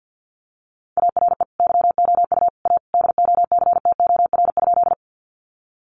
Morse code, heard as a keyed tone, about 50 wpm, 700 Hz. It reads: ARE YOU A DOCTOR?
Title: Sentences from Top 1500 Words 50wpm